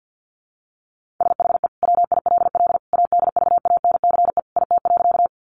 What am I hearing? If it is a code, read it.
Morse code, heard as a keyed tone, about 48 wpm, 700 Hz. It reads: SHE WILL ADVANCE IT.
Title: Sentences from Top 1400 Words 50wpm